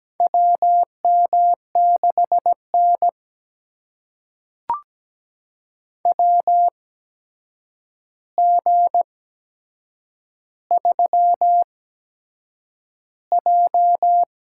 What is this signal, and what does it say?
Morse code, heard as a keyed tone, about 17 wpm, spaced as for 7 wpm, 700 Hz. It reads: WM6N WG3J
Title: USA Call Signs - 8x Character Spacing - 17wpm